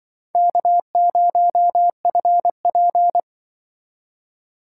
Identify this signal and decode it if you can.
Morse code, heard as a keyed tone, about 25 wpm, 700 Hz. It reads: K0FP
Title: Speed Racing - USA 1x2 Call Signs 20wpm